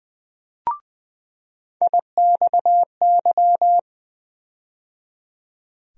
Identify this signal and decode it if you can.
Morse code, heard as a keyed tone, about 20 wpm, 700 Hz. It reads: IXY
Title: USA Call Sign Postfixes - 3 Letters 20wpm